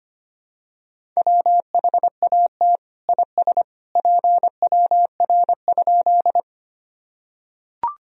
Morse code, heard as a keyed tone, about 25 wpm, 700 Hz. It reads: WHAT IS PWR?
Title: Sentences from Top 500 Words and Abbreviations 25wpm